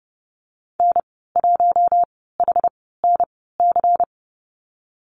Morse code, heard as a keyed tone, about 30 wpm, spaced as for 17 wpm, 700 Hz. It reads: N1HNC
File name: Lesson 19g - Letter w - Call Signs 30wpm